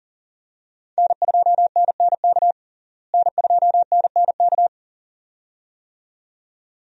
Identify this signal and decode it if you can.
Morse code, heard as a keyed tone, about 40 wpm, 700 Hz. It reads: N2NNK N2NNK
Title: USA Call Signs - 5 Characters - Repeated 2x - 40wpm